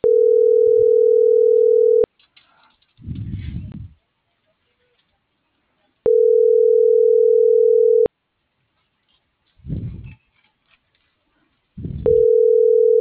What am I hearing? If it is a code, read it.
Ringback tone (North America), 440 + 480 Hz.